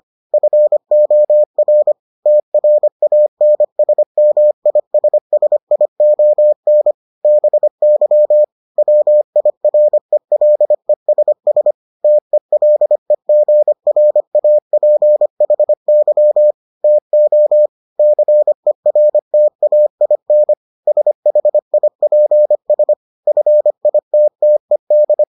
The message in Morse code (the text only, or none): FOR TRANSMISSION BY WIRELESS TELEGRAPHY TO CERTAIN SHIPS FITTED